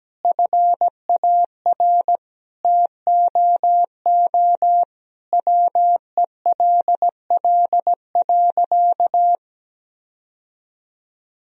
Morse code, heard as a keyed tone, about 17 wpm, 700 Hz. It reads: FAR TOO WELL.
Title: Sentences from Top 1000 Words - Review Part 3 - 17wpm